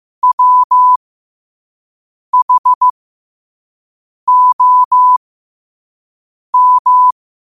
Morse code, heard as a keyed tone, about 15 wpm, spaced as for 5 wpm, 1000 Hz. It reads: WHOM
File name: qotdfiles / 5wpm